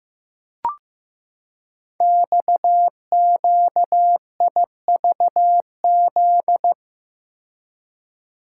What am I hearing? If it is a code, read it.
Morse code, heard as a keyed tone, about 15 wpm, 700 Hz. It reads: XQIVZ
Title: Code Groups - 5 Letters - 15wpm